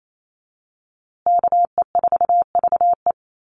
Morse code, heard as a keyed tone, about 28 wpm, 690 Hz. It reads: KE4VE
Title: USA Call Signs Repeated 3x 28wpm